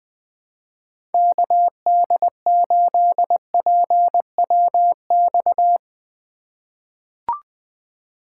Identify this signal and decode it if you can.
Morse code, heard as a keyed tone, about 20 wpm, 700 Hz. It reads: KD8PWX
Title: USA Call Signs - 2x Character Spacing - 20wpm